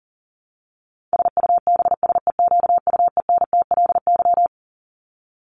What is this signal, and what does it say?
Morse code, heard as a keyed tone, about 40 wpm, 700 Hz. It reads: SUBSEQUENTLY
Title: LY Suffix - Top 100 Words with Base Words 40wpm